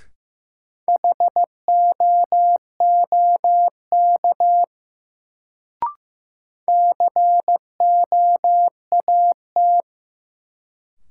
Morse code, heard as a keyed tone, about 15 wpm, 700 Hz. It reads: HOOK COAT
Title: Sentences from Top 1600 Words - New Word Drill - 15wpm